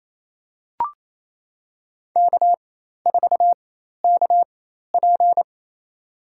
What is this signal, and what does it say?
Morse code, heard as a keyed tone, about 28 wpm, spaced as for 13 wpm, 700 Hz. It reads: K4KP
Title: USA Call Signs - 4 Characters - 4x Character Spacing - 28wpm